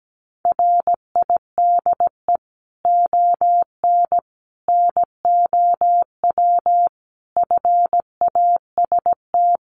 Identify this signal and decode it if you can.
Morse code, heard as a keyed tone, about 17 wpm, 700 Hz. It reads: RIDE ON NOW FAST
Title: Sentences from Top 1500 Words 17wpm